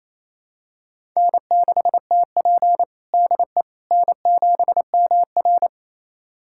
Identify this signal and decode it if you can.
Morse code, heard as a keyed tone, about 28 wpm, 700 Hz. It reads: N6TP DE N7MR
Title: USA Callsign DE Callsign - 4 Characters - 28wpm